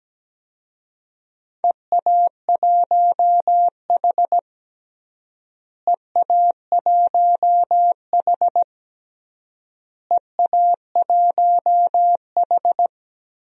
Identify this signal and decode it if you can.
Morse code, heard as a keyed tone, about 17 wpm, 700 Hz. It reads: EA1H EA1H EA1H
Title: DXCC-100 Call Signs - 4 Characters - Repeated 3x - 17wpm